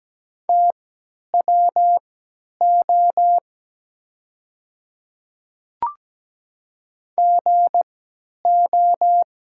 Morse code, heard as a keyed tone, about 17 wpm, spaced as for 10 wpm, 700 Hz. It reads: TWO GO
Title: 3x Character Spacing - Top 100 Words - 17wpm